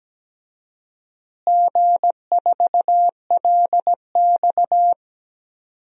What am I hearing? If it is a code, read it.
Morse code, heard as a keyed tone, about 17 wpm, 700 Hz. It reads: G4LX
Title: DXCC-100 Call Signs - 4 Characters - Repeated 3x - 17wpm